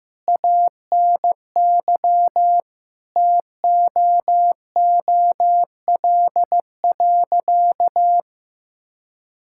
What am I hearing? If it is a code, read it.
Morse code, heard as a keyed tone, about 15 wpm, 700 Hz. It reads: ANY TOOL.
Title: Sentences from Top 1000 Words - Review Part 6 - 15wpm